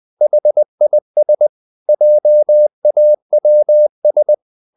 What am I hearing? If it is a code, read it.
Morse code, heard as a keyed tone, about 20 wpm, 600 Hz. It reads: HIS JAWS
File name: the old man and the sea part 2 20wpm